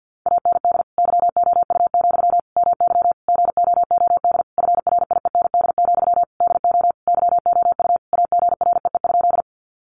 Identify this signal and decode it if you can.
Morse code, heard as a keyed tone, about 48 wpm, 700 Hz. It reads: AND YOU, MY GOOD FRIEND, DO YOU AGREE?
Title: Sentences from Top 1100 Words 50wpm